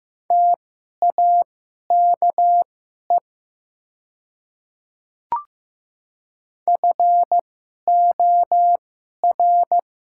Morse code, heard as a keyed tone, about 15 wpm, spaced as for 11 wpm, 700 Hz. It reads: TAKE FOR
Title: 2x Character Spacing - Top 200 Words - 15wpm